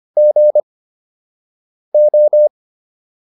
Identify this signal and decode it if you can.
Morse code, heard as a keyed tone, about 25 wpm, spaced as for 6 wpm, 600 Hz. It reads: GO